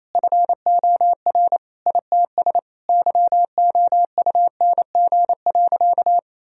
Morse code, heard as a keyed tone, about 28 wpm, 700 Hz. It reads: FOR ITS YOUNG.
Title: Sentences from Top 1000 Words - Review Part 2 - 28wpm